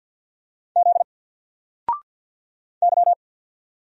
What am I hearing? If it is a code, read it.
Morse code, heard as a keyed tone, about 50 wpm, 700 Hz. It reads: Z Y
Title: Single Character (Letter:Number:Punctuation:Prosign) 50wpm